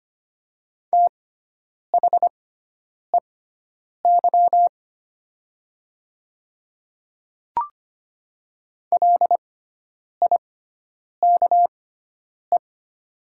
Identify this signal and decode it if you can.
Morse code, heard as a keyed tone, about 25 wpm, spaced as for 9 wpm, 700 Hz. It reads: THEY LIKE
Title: 6x Character Spacing - Top 100 Words - 25wpm